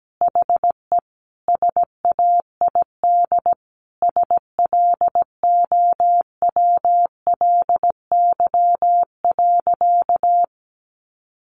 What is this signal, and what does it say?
Morse code, heard as a keyed tone, about 17 wpm, 700 Hz. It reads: HE SAID SLOWLY.
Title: Sentences from Top 1100 Words 17wpm